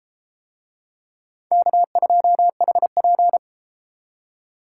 Morse code, heard as a keyed tone, about 30 wpm, 700 Hz. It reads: K2HP